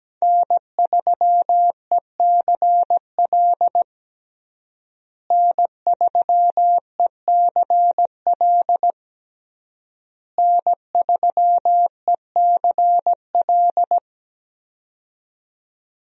Morse code, heard as a keyed tone, about 17 wpm, 700 Hz. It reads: N3ECL N3ECL N3ECL
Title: USA Call Signs - 5 Characters - Repeated 3x - 17wpm